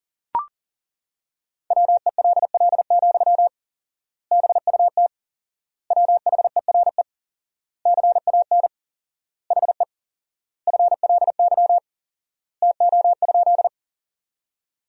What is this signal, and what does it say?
Morse code, heard as a keyed tone, about 40 wpm, 700 Hz. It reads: WELL, BUT WHERE CAN HE FLY TO?
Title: 4x Word-Spacing - Sentences from Top 500 Words - 40wpm